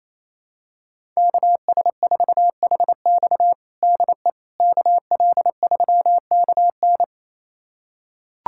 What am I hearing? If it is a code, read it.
Morse code, heard as a keyed tone, about 28 wpm, 700 Hz. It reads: KS4HX DE KL3KN
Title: USA Callsign DE Callsign - 5 Characters - 28wpm